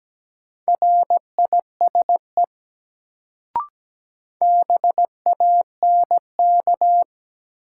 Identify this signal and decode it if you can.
Morse code, heard as a keyed tone, about 17 wpm, 700 Hz. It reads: RISE BANK
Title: Sentences from Top 900 Words - New Word Drill - 17wpm